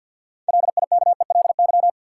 Morse code, heard as a keyed tone, about 50 wpm, 700 Hz. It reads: LIKELY
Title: Sentences from Top 600 Words - New Word Drill - 50wpm